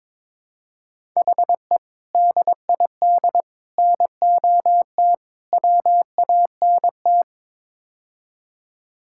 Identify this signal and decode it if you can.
Morse code, heard as a keyed tone, about 22 wpm, 700 Hz. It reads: HE DID NOT WANT
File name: practice / 22wpm / Build A Phrase - 5-Word Phrases - 22wpm